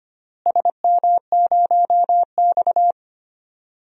Morse code, heard as a keyed tone, about 25 wpm, 700 Hz. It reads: SM0X
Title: DXCC-100 Call Signs - 4 Characters - Repeated 2x - 25wpm